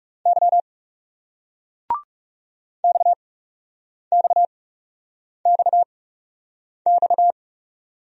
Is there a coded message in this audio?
Yes: Morse code from 0.3 to 7.3 s, heard as a keyed tone, about 39 wpm, spaced as for 8 wpm, 700 Hz. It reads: Y XXXX